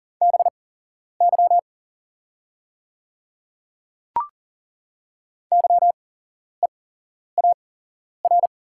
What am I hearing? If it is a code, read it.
Morse code, heard as a keyed tone, about 39 wpm, spaced as for 11 wpm, 690 Hz. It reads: BY YEAR
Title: 8x Character Spacing - Top 500 Words - 40wpm